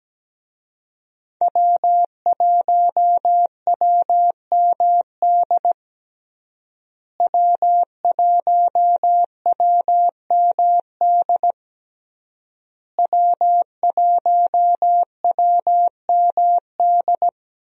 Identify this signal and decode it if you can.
Morse code, heard as a keyed tone, about 17 wpm, 700 Hz. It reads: W1WMD W1WMD W1WMD
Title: USA Call Signs - 5 Characters - Repeated 3x - 17wpm